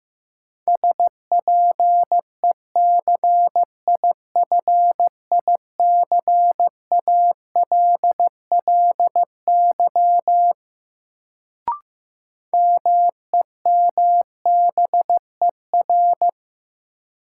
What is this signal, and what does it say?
Morse code, heard as a keyed tone, about 15 wpm, 700 Hz. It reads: SPECIFICALLY MEMBER
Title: Sentences from Top 1100 Words - New Word Drill Alt - 15wpm